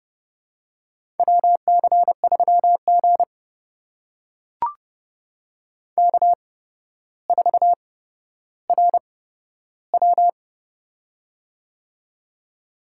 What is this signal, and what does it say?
Morse code, heard as a keyed tone, about 30 wpm, spaced as for 13 wpm, 700 Hz. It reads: WC3G K4RW
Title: USA Call Signs - 4 Characters - 8x Character Spacing - 30wpm